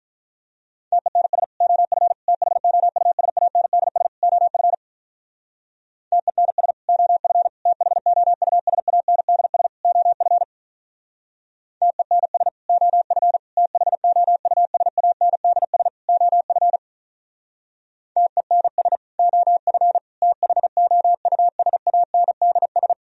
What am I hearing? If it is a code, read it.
Morse code, heard as a keyed tone, about 46 wpm, 700 Hz. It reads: TENS OF THOUSANDS OF TENS OF THOUSANDS OF TENS OF THOUSANDS OF TENS OF THOUSANDS